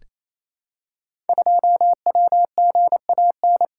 Morse code, heard as a keyed tone, about 27 wpm, 700 Hz. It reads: 2WGAN
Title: Code Groups - 5 Letter:Numbers - 28wpm